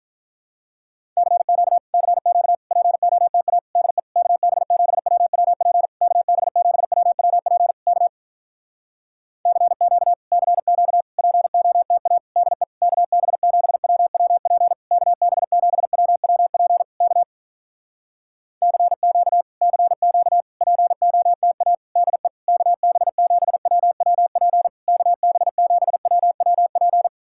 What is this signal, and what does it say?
Morse code, heard as a keyed tone, about 48 wpm, 700 Hz. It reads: CQ CQ POTA DE KB7WWP KB7WWP K CQ CQ POTA DE KB7WWP KB7WWP K CQ CQ POTA DE KB7WWP KB7WWP